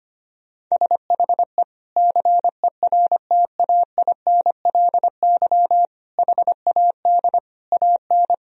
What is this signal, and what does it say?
Morse code, heard as a keyed tone, about 25 wpm, 700 Hz. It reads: SHE CERTAINLY HAD AN
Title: Sentences from Top 1000 Words - Review Part 4 - 25wpm